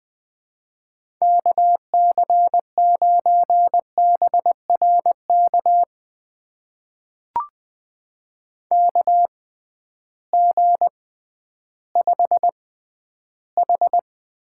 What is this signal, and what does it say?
Morse code, heard as a keyed tone, about 20 wpm, spaced as for 12 wpm, 700 Hz. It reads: KC9BRK KG5H
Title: USA Call Signs - 6 Characters - 6x Character Spacing - 20wpm